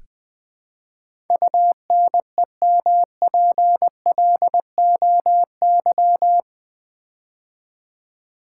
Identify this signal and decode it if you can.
Morse code, heard as a keyed tone, about 20 wpm, 700 Hz. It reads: UNEMPLOY
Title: Build A Word - 4 Letters At A Time - 12-Letter Words - 20wpm